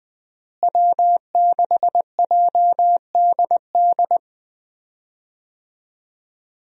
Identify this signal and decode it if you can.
Morse code, heard as a keyed tone, about 20 wpm, 700 Hz. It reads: W6JDD
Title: USA Call Signs - 5 Characters - Repeated 2x - 20wpm